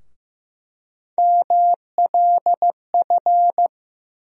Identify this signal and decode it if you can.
Morse code, heard as a keyed tone, about 15 wpm, 700 Hz. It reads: MLF
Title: USA Call Sign Postfixes - 3 Letters 15wpm